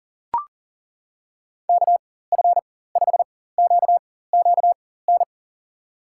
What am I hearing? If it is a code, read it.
Morse code, heard as a keyed tone, about 40 wpm, spaced as for 19 wpm, 700 Hz. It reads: KF5QQN
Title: USA Call Signs - 6 Characters - 4x Character Spacing - 40wpm